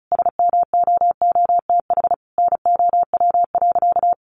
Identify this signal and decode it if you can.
Morse code, heard as a keyed tone, about 34 wpm, 700 Hz. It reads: SMOOTH NOW.
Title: Sentences from Top 1400 Words 35wpm